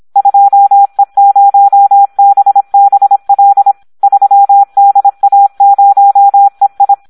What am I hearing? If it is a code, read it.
Morse code, heard as a keyed tone, about 26 wpm, 800 Hz. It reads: 2E0BBL 3DA0EI
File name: Audio practica CW / SD 1 26